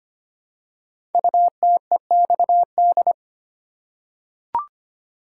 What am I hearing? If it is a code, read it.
Morse code, heard as a keyed tone, about 25 wpm, 700 Hz. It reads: UTEXD
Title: Code Groups - 5 Letters - 25wpm